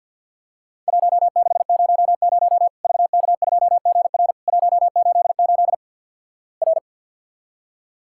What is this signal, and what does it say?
Morse code, heard as a keyed tone, about 50 wpm, 700 Hz. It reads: 1600 VK2GR 188 R
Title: Worked All Europe DX Contest - QTC Exchange - 50wpm